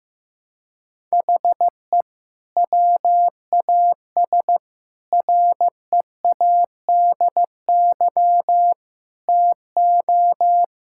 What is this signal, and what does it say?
Morse code, heard as a keyed tone, about 15 wpm, 700 Hz. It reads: HE WAS READY TO
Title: Sentences from Top 1000 Words - Review Part 5 - 15wpm